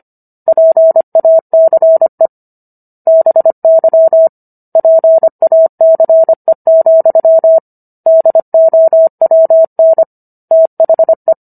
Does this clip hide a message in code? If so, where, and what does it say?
Yes: Morse code from 0.5 to 11.3 s, heard as a keyed tone, about 25 wpm, 650 Hz. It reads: PACE BY PACE, DOWN THE